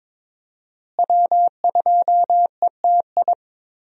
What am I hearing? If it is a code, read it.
Morse code, heard as a keyed tone, about 22 wpm, 700 Hz. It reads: W2ETI
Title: USA Call Signs - 5 Characters - Repeated 2x - 22wpm